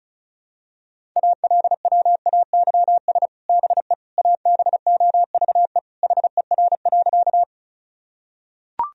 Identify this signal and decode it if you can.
Morse code, heard as a keyed tone, about 34 wpm, 700 Hz. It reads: ALWAYS BE ABOVE HER.
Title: Sentences from Top 1000 Words - Review Part 2 - 35wpm